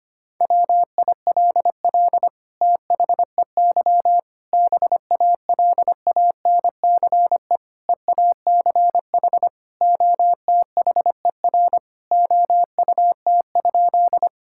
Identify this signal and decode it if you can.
Morse code, heard as a keyed tone, about 25 wpm, 700 Hz. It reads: WILL THEY BALANCE EACH OTHER OUT?
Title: Sentences from Top 1000 Words - Review Part 6 - 25wpm